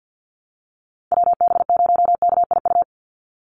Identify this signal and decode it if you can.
Morse code, heard as a keyed tone, about 49 wpm, 700 Hz. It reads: WB0KIU
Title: NA SOTA Chaser Callsigns 50wpm